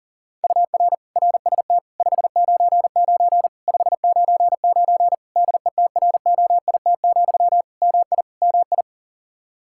Morse code, heard as a keyed tone, about 40 wpm, 700 Hz. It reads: UR RST 599 599 DETROIT, MI MI
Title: Practice QSO - Exchange 1 - Beginning - 40wpm